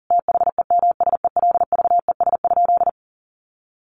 Morse code, heard as a keyed tone, about 40 wpm, 700 Hz. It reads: THEMSELVES?
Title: Sentences from Top 800 Words 40wpm